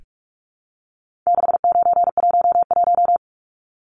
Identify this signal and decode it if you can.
Morse code, heard as a keyed tone, about 44 wpm, 700 Hz. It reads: 6911
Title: Numbers - 4-Digit 45wpm